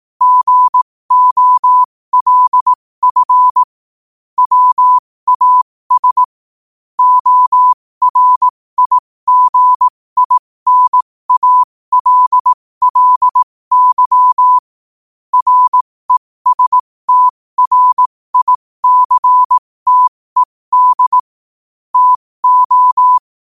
Morse code, heard as a keyed tone, about 18 wpm, spaced as for 15 wpm, 1000 Hz. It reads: GOLF WAS ORIGINALLY RESTRICTED TO
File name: qotdfiles / 15wpm